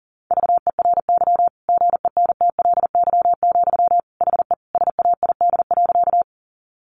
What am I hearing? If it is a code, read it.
Morse code, heard as a keyed tone, about 40 wpm, 700 Hz. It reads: VERY GENTLY, HE SAID.
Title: Sentences from Top 1500 Words 40wpm